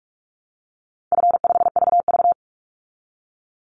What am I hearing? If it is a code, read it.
Morse code, heard as a keyed tone, about 44 wpm, 700 Hz. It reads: F5VV
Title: Easy Call Signs Repeated 3x - 45wpm